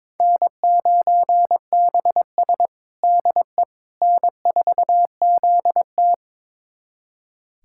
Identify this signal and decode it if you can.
Morse code, heard as a keyed tone, about 22 wpm, 700 Hz. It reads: N9BS DE N4ZT
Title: USA Callsign DE Callsign - 4 Characters - 22wpm